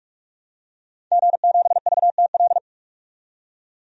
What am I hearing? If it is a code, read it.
Morse code, heard as a keyed tone, about 46 wpm, 700 Hz. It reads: G7VTL
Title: DXCC-100 Call Signs - 5 Characters - 45wpm